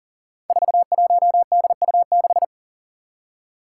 Morse code, heard as a keyed tone, about 40 wpm, 700 Hz. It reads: 41DU6
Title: Five Challenging Letter-Number Combinations 40wpm